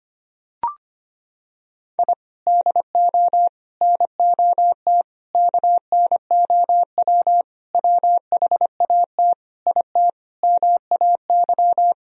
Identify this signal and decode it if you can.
Morse code, heard as a keyed tone, about 25 wpm, 700 Hz. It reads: I DO NOT KNOW WHAT IT MAY
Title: Sentences from Top 1000 Words - Review Part 2 - 25wpm